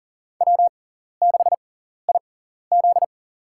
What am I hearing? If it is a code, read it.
Morse code, heard as a keyed tone, about 40 wpm, spaced as for 14 wpm, 700 Hz. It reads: W6IZ
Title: USA Call Signs - 4 Characters - 6x Character Spacing - 40wpm